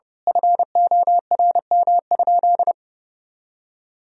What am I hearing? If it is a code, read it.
Morse code, heard as a keyed tone, about 32 wpm, 700 Hz. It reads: FORM?
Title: Sentences from Top 1000 Words - Review Part 1 - 30wpm